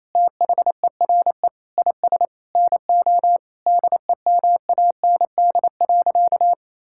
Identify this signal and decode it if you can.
Morse code, heard as a keyed tone, about 28 wpm, 700 Hz. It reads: THERE IS NO DEMAND.
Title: Sentences from Top 1000 Words - Review Part 5 - 28wpm